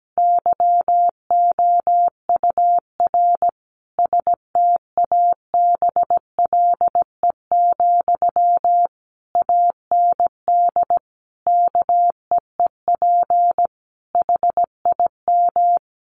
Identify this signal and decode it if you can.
Morse code, heard as a keyed tone, about 17 wpm, 700 Hz. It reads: YOUR STABLE, AND KEEP HIM
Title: Sentences from Top 1500 Words 17wpm